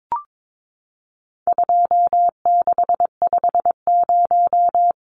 Morse code, heard as a keyed tone, about 22 wpm, 700 Hz. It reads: 2650
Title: Numbers - 4-Digit 22wpm